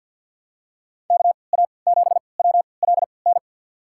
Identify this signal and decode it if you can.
Morse code, heard as a keyed tone, about 50 wpm, spaced as for 28 wpm, 700 Hz. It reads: KA7WLN
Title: USA Call Signs - 3x Character Spacing - 50wpm